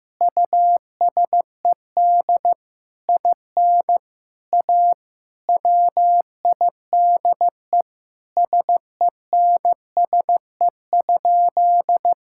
Morse code, heard as a keyed tone, about 15 wpm, 700 Hz. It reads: USED IN A WIDE SENSE?
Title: Sentences from Top 1000 Words - Review Part 6 - 15wpm